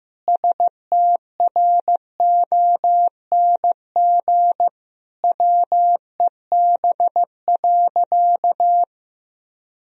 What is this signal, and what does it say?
Morse code, heard as a keyed tone, about 15 wpm, 700 Hz. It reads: STRONG WEB.
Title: Sentences from Top 1000 Words - Review Part 4 - 15wpm